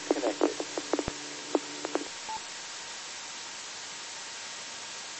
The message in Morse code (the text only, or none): none